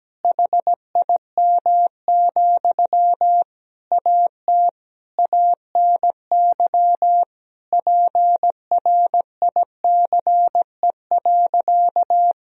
Morse code, heard as a keyed tone, about 17 wpm, 700 Hz. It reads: HIM, AT ANY PRICE.
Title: Sentences from Top 1000 Words - Review Part 4 - 17wpm